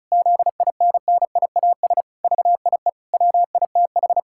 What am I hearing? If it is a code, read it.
Morse code, heard as a keyed tone, about 36 wpm, 700 Hz. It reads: ZINNIAS VIE WITH
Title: Pangram Challenge v2 - 35wpm